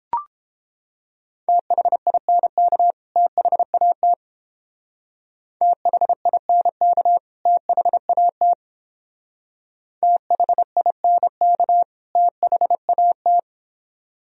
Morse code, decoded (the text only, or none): THINK THAT THINK THAT THINK THAT